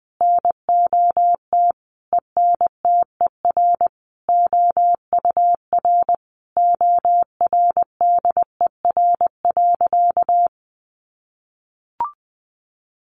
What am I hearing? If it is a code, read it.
Morse code, heard as a keyed tone, about 20 wpm, 700 Hz. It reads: NOT ENTER OUR ORDER.
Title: Sentences from Top 1300 Words 20wpm